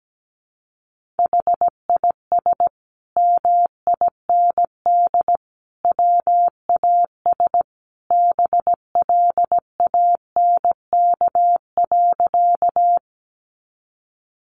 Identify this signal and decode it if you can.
Morse code, heard as a keyed tone, about 17 wpm, 700 Hz. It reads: HIS MIND WAS BLANK.
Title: Sentences from Top 1500 Words 17wpm